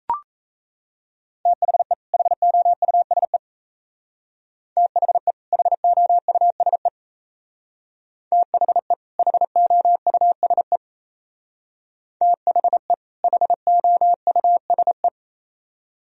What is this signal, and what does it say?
Morse code, heard as a keyed tone, about 33 wpm, 690 Hz. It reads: THE HOUSE THE HOUSE THE HOUSE THE HOUSE
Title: Speed-Racing - Sets of 2 Words 28wpm